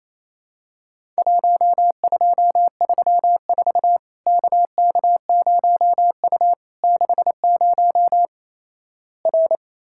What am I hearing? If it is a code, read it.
Morse code, heard as a keyed tone, about 28 wpm, 700 Hz. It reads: 1234 KK0U 60 R
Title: Worked All Europe DX Contest - QTC Exchange - 28wpm